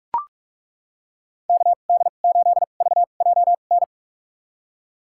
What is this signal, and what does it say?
Morse code, heard as a keyed tone, about 41 wpm, spaced as for 31 wpm, 700 Hz. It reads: KD8VJN
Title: USA Call Signs - 6 Characters - 2x Character Spacing - 45wpm